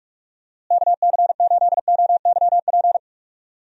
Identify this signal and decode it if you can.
Morse code, heard as a keyed tone, about 46 wpm, 700 Hz. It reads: KC8OYP
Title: USA Call Signs - 6 Characters - Repeated 2x - 45wpm